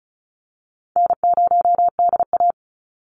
Morse code, heard as a keyed tone, about 34 wpm, 700 Hz. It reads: N0DA
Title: NA SOTA Chaser Callsigns 35wpm